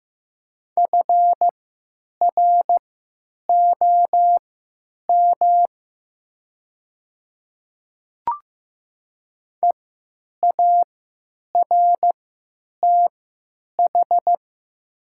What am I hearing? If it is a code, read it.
Morse code, heard as a keyed tone, about 15 wpm, spaced as for 9 wpm, 700 Hz. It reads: FROM EARTH
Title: 3x Character Spacing - Top 300 Words - 15wpm